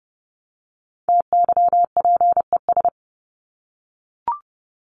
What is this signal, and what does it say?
Morse code, heard as a keyed tone, about 30 wpm, 700 Hz. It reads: TYPES
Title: ES:S Suffix - Top 100 Words with Base Words 30wpm